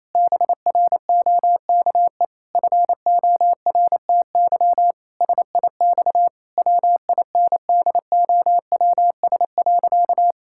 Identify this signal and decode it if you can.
Morse code, heard as a keyed tone, about 27 wpm, 700 Hz. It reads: BROKE FORTY SIX WINDOWS.
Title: Pangram Challenge v2 - 28wpm